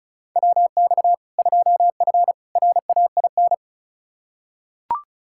Morse code, heard as a keyed tone, about 36 wpm, 700 Hz. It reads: WX 2F RAIN
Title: K1USN SST QSO Practice 35wpm